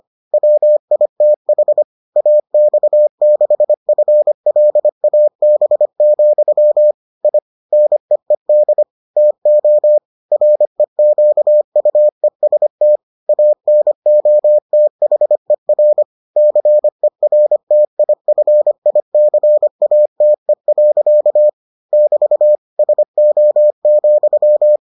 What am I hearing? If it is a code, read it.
Morse code, heard as a keyed tone, about 25 wpm, 600 Hz. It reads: WITH AX6FLAB, I NEED TO REQUEST ANOTHER CERTIFICATE. <BT> SO,